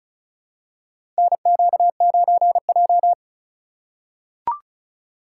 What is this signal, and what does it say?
Morse code, heard as a keyed tone, about 34 wpm, 700 Hz. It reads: NQ9J
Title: USA Call Signs - 4 Characters - 4x Character Spacing - 35wpm